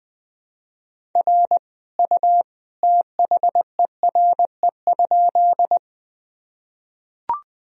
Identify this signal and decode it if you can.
Morse code, heard as a keyed tone, about 20 wpm, 700 Hz. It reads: R U THERE?
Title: Sentences from Top 100 Words and Abbreviations 20wpm